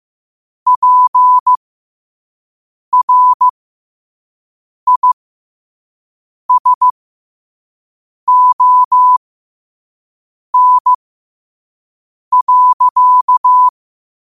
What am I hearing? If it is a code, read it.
Morse code, heard as a keyed tone, about 15 wpm, spaced as for 5 wpm, 1000 Hz. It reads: PRISON.